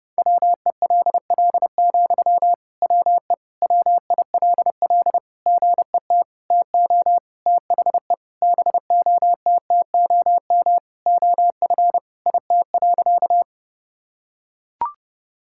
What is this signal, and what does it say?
Morse code, heard as a keyed tone, about 28 wpm, 700 Hz. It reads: WELL, WE WILL GET TO THE BOTTOM OF IT.
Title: Sentences from Top 1000 Words - Review Part 6 - 30wpm